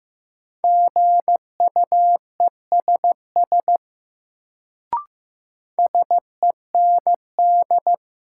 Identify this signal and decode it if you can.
Morse code, heard as a keyed tone, about 15 wpm, 700 Hz. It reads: GUESS SEND
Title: Sentences from Top 1500 Words - New Word Drill Alt - 15wpm